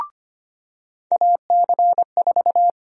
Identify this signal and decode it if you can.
Morse code, heard as a keyed tone, about 25 wpm, 700 Hz. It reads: AC4
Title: USA Call Sign Prefixes - 25wpm